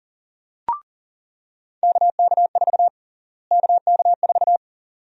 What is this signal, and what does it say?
Morse code, heard as a keyed tone, about 37 wpm, 700 Hz. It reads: KK4 KK4
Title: USA Call Sign Prefixes - Repeated 2x - 40wpm